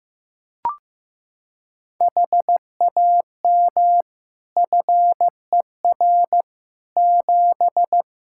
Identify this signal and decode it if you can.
Morse code, heard as a keyed tone, about 15 wpm, 700 Hz. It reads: HAM FER 7
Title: Practice QSO - Exchange 3 - More Info - 15wpm